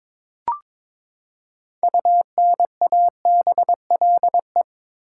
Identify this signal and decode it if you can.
Morse code, heard as a keyed tone, about 21 wpm, 700 Hz. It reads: UNABLE
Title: Six Letter Words - 22wpm